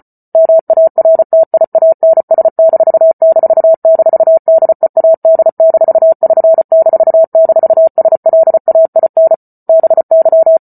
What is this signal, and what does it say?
Morse code, heard as a keyed tone, about 34 wpm, 650 Hz. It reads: MARTIANS---DEAD-<SN>--SLAIN BY